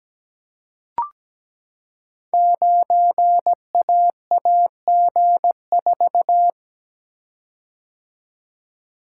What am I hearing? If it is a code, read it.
Morse code, heard as a keyed tone, about 17 wpm, 700 Hz. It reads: 9AAG4
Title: Code Groups - 5 Letter:Numbers - 17wpm